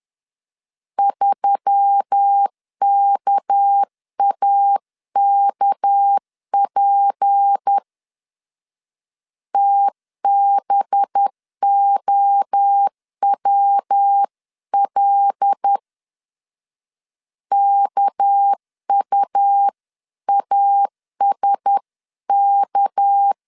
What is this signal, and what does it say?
Morse code, heard as a keyed tone, about 10 wpm, 800 Hz. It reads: 3KAKP TBOWL KUASK
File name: LEZIONI / 6 PKX0 / 6 PK 10wpm 2436